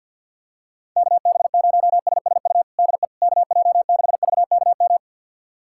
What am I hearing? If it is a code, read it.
Morse code, heard as a keyed tone, about 46 wpm, 700 Hz. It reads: KB0SSU DE KJ6VKM
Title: USA Callsign DE Callsign - 6 Characters - 50wpm